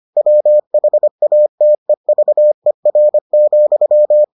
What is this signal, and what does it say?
Morse code, heard as a keyed tone, about 25 wpm, 600 Hz. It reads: WHATEVER,